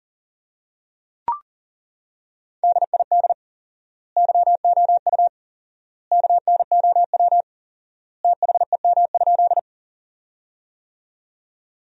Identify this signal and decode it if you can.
Morse code, heard as a keyed tone, about 37 wpm, 700 Hz. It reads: DID YOU KNOW THEM?
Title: 4x Word-Spacing - Sentences from Top 100 Words - 40wpm